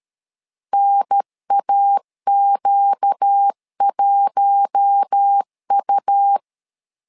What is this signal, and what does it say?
Morse code, heard as a keyed tone, about 13 wpm, 800 Hz. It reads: NAQ1U
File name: LEZIONI / 5 LBVC / 5 LB 12wpm 1906